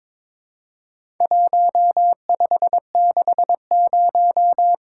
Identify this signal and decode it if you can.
Morse code, heard as a keyed tone, about 23 wpm, 700 Hz. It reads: 1560
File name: practice / 22wpm / Four Letter-Number Combinations - 22wpm